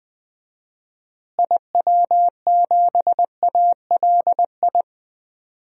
Morse code, heard as a keyed tone, about 20 wpm, 700 Hz. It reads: IW7ALI
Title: DXCC-100 Call Signs - 6 Characters - Repeated 3x - 20wpm